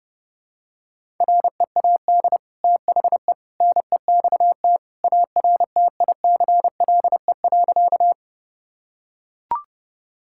Morse code, heard as a keyed tone, about 30 wpm, 700 Hz. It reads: READ THE NEXT ARTICLE.
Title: Sentences from Top 1000 Words - Review Part 5 - 30wpm